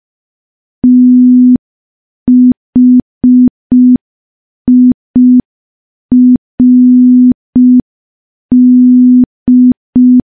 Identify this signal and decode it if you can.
Morse code, heard as a keyed tone, about 5 wpm, 250 Hz. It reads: THIRD
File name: morse / morse-training-05WPM